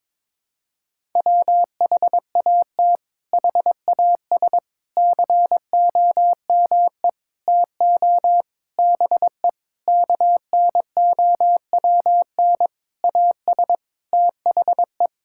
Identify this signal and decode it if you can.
Morse code, heard as a keyed tone, about 22 wpm, 700 Hz. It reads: WHAT HAS COME TO BE KNOWN AS THE
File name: practice / 22wpm / Sentences with 8-Word Phrases - 22wpm